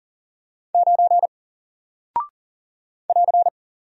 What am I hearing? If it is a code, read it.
Morse code, heard as a keyed tone, about 39 wpm, 700 Hz. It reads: 9 <AR>
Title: Single Character (Letter:Number:Punctuation:Prosign) 40wpm